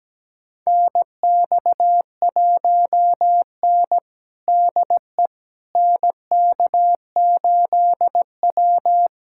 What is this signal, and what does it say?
Morse code, heard as a keyed tone, about 17 wpm, 700 Hz. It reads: NX1N DE NK8W
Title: USA Callsign DE Callsign - 4 Characters - 17wpm